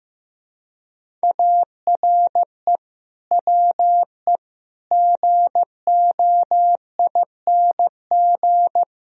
Morse code, heard as a keyed tone, about 15 wpm, 700 Hz. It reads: ARE WE GOING
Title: Speed-Racing - Sets of 4 Words 15wpm